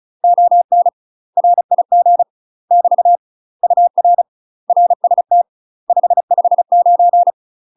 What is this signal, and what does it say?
Morse code, heard as a keyed tone, about 35 wpm, 700 Hz. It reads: ON RIG <BT> UR RST 559
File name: UK MASTER QSOs TRAINING / 35WPM / UK MASTER QSOs 0005